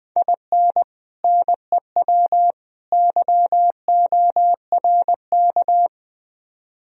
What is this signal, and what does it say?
Morse code, heard as a keyed tone, about 19 wpm, 700 Hz. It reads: IN NEW YORK
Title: Speed-Racing - Sets of 3 Words 17wpm